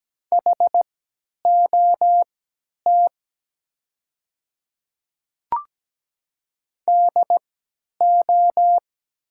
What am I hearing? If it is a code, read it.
Morse code, heard as a keyed tone, about 17 wpm, spaced as for 10 wpm, 700 Hz. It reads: HOT DO
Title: 3x Character Spacing - Top 100 Words - 17wpm